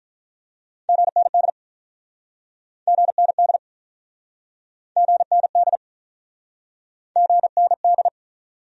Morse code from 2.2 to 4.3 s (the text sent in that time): GND